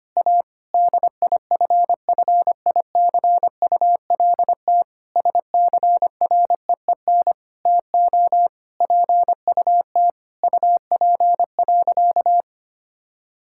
Morse code, heard as a keyed tone, about 25 wpm, 700 Hz. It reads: A DIFFICULT SCREEN TO PUT UP.
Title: Sentences from Top 1000 Words - Review Part 6 - 25wpm